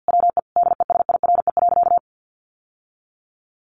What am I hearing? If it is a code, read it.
Morse code, heard as a keyed tone, about 50 wpm, 700 Hz. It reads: WE DESIRE.
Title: Sentences from Top 1300 Words 50wpm